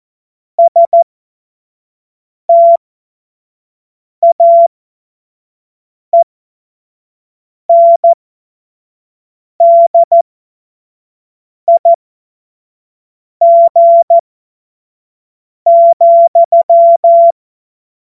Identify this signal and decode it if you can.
Morse code, heard as a keyed tone, about 14 wpm, spaced as for 5 wpm, 680 Hz. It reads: STAENDIG,
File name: Morsekurs / Witze / Witz 05 März 2024 25 F70